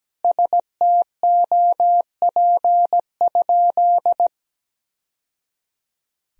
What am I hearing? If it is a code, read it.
Morse code, heard as a keyed tone, about 17 wpm, 700 Hz. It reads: STOP?
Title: Sentences from Top 1000 Words - Review Part 1 - 17wpm